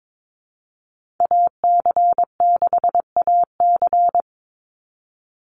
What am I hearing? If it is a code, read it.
Morse code, heard as a keyed tone, about 22 wpm, 700 Hz. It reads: AC6AC
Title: Call Signs - 22wpm